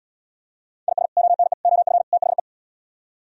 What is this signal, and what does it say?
Morse code, heard as a keyed tone, about 77 wpm, 700 Hz. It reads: IS ONE OF THE